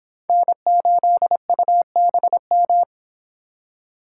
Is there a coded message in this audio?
Yes: Morse code from 0.3 to 2.8 s, heard as a keyed tone, about 25 wpm, 700 Hz. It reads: N8UBM